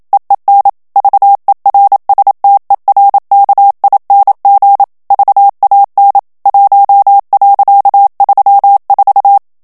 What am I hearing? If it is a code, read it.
Morse code, heard as a keyed tone, about 27 wpm, 800 Hz. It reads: EEN VERSTERKING VAN 1.34
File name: HOLLANDAIS texte clair 1 vitesse 26 mots minute